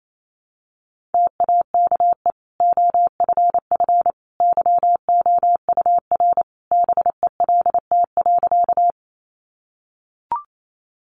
Morse code, heard as a keyed tone, about 28 wpm, 700 Hz. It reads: TAKE OFF YOUR BELT.
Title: Sentences from Top 1400 Words 28wpm